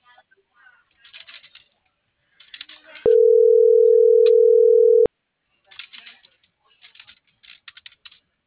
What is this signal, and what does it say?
Ringback tone (North America), 440 + 480 Hz.